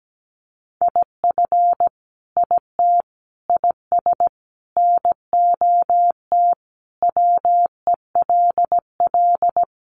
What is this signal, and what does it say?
Morse code, heard as a keyed tone, about 17 wpm, 700 Hz. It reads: IF IT IS NOT WELL
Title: Sentences from Top 1200 Words 17wpm